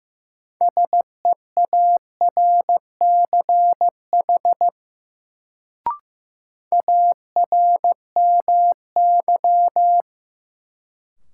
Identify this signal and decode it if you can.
Morse code, heard as a keyed tone, about 15 wpm, 700 Hz. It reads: SEARCH ARMY
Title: Sentences from Top 800 Words - New Word Drill - 15wpm